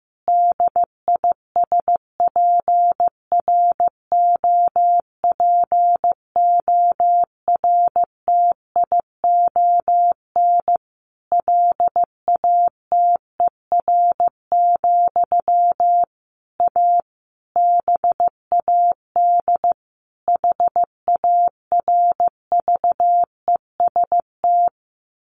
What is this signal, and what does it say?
Morse code, heard as a keyed tone, about 15 wpm, 700 Hz. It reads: DISPROPORTION LATER, A BAD HARVEST
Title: A Sound Of Thunder 15wpm